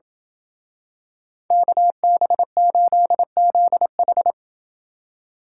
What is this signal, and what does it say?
Morse code, heard as a keyed tone, about 28 wpm, 700 Hz. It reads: KB8ZH